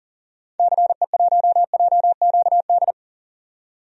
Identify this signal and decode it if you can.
Morse code, heard as a keyed tone, about 40 wpm, 700 Hz. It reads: CE1JQD
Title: DXCC-100 Call Signs - 6 Characters - 40wpm